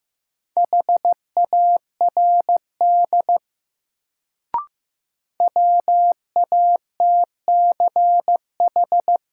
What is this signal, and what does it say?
Morse code, heard as a keyed tone, about 15 wpm, 700 Hz. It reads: HARD WATCH
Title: Sentences from Top 300 Words - New Word Drill - 15wpm